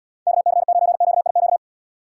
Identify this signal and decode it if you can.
Morse code, heard as a keyed tone, about 79 wpm, 700 Hz. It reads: KC9QIQ